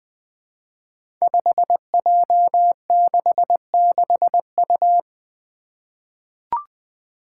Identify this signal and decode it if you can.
Morse code, heard as a keyed tone, about 20 wpm, 700 Hz. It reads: 5J66U
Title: Five Challenging Letter-Number Combinations 20wpm